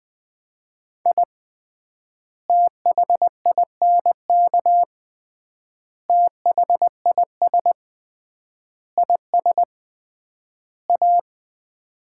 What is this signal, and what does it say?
Morse code, heard as a keyed tone, about 20 wpm, 700 Hz. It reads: I THINK THIS IS A
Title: 3x Word-Spacing - Sets of 5 Words - 20wpm